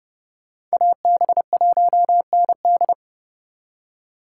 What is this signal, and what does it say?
Morse code, heard as a keyed tone, about 38 wpm, 700 Hz. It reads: AB1ND